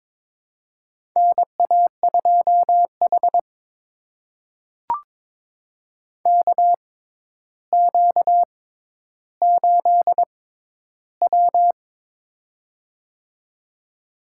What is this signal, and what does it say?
Morse code, heard as a keyed tone, about 22 wpm, spaced as for 11 wpm, 700 Hz. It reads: NA2H KQ8W
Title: USA Call Signs - 4 Characters - 6x Character Spacing - 22wpm